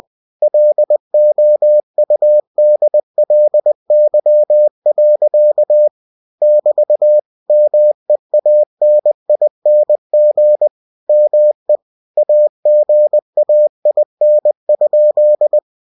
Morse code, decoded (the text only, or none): LOUDLY. <BT> MEANING ME AGAIN?